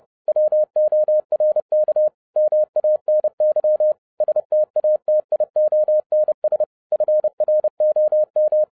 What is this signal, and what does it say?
Morse code, heard as a keyed tone, about 30 wpm, 610 Hz. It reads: WORK MANY STATIONS FROM
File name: 2012-01-mp3 / n11-I DIDNT MISS THE SOLAR CYCLE 24-30wpm